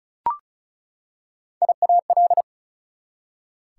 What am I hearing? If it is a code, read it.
Morse code, heard as a keyed tone, about 36 wpm, 700 Hz. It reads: IAL
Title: Prefixes and Suffixes 35wpm